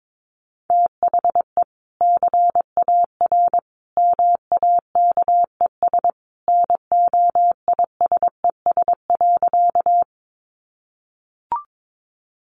Morse code, heard as a keyed tone, about 22 wpm, 700 Hz. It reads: THE CAR MAKES NOISES.
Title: Easy Phrases 22wpm